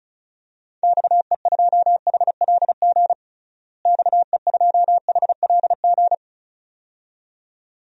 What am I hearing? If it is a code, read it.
Morse code, heard as a keyed tone, about 34 wpm, 700 Hz. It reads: XE2HLG XE2HLG
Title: DXCC-100 Call Signs - 6 Characters - Repeated 2x - 35wpm